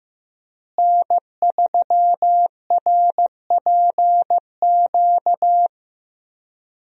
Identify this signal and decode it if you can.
Morse code, heard as a keyed tone, about 15 wpm, 700 Hz. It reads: N3RPQ
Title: USA Call Signs - 5 Characters - Repeated 3x - 15wpm